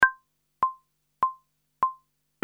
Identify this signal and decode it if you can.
There is no code.